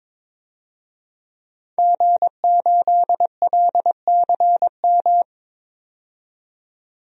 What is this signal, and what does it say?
Morse code, heard as a keyed tone, about 22 wpm, 700 Hz. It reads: G8LCM
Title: DXCC-100 Call Signs - 5 Characters - 22wpm